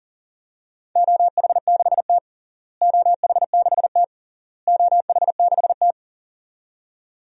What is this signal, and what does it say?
Morse code, heard as a keyed tone, about 37 wpm, 700 Hz. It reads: OH6T OH6T OH6T
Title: DXCC-100 Call Signs - 4 Characters - Repeated 3x - 40wpm